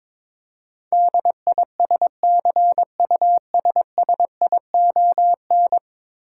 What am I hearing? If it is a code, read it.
Morse code, heard as a keyed tone, about 22 wpm, 700 Hz. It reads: DISCUSSION
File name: 10 Letter Words - First 3 Letters Given - 22wpm